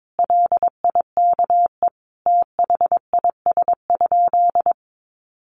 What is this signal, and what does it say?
Morse code, heard as a keyed tone, about 22 wpm, 700 Hz. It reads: LIKE THIS?
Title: Sentences from Top 1200 Words 22wpm